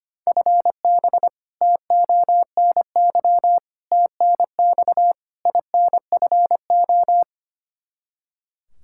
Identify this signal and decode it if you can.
Morse code, heard as a keyed tone, about 25 wpm, 700 Hz. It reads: FB TONY TNX INFO
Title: Practice QSO - Exchange 3 - More Info - 25wpm